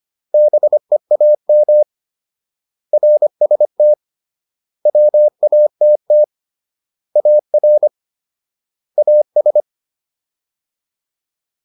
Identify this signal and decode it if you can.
Morse code, heard as a keyed tone, about 25 wpm, 600 Hz. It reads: BEAM RST WATT AR AS